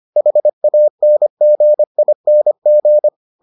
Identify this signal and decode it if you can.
Morse code, heard as a keyed tone, about 25 wpm, 600 Hz. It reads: HANGING